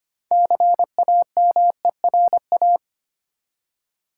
Morse code, heard as a keyed tone, about 25 wpm, 700 Hz. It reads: CAMERA
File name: practice / 25wpm / Easy Words with Hints - 25wpm